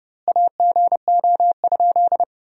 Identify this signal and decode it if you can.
Morse code, heard as a keyed tone, about 28 wpm, 700 Hz. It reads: AGO?
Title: 2x Word-Spacing - Sentences from Top 500 Words - 30wpm